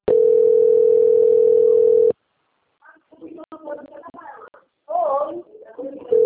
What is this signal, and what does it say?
Ringback tone (North America), 440 + 480 Hz, with speech.
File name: rg-1600-unknown- - - .37093